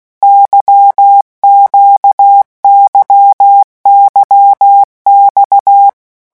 Morse code, heard as a keyed tone, about 16 wpm, 800 Hz. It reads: YQYYX
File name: morse mp3 / revision1-vitesse 15 mots